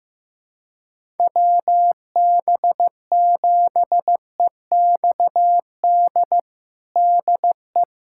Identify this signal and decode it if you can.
Morse code, heard as a keyed tone, about 15 wpm, 700 Hz. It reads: WB7EXD DE
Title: USA Callsign DE Callsign 15wpm